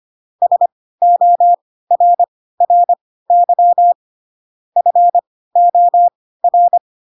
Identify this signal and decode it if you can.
Morse code, heard as a keyed tone, about 25 wpm, spaced as for 16 wpm, 700 Hz. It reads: SORRY FOR
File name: Pinocchio / Pinocchio ch4 10 25 ik1hge